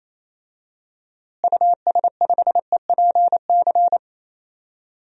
Morse code, heard as a keyed tone, about 28 wpm, 700 Hz. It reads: US5EPC
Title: DXCC-100 Call Signs - 6 Characters - Repeated 3x - 28wpm